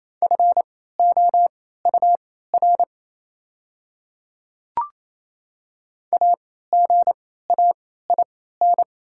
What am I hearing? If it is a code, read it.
Morse code, heard as a keyed tone, about 28 wpm, spaced as for 16 wpm, 690 Hz. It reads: FOUR AGAIN
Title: 3x Character Spacing - Top 500 Words - 28wpm